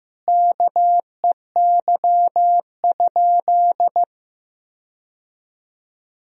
Morse code, heard as a keyed tone, about 16 wpm, 700 Hz. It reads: KEY?
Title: Sentences from Top 1000 Words - Review Part 4 - 15wpm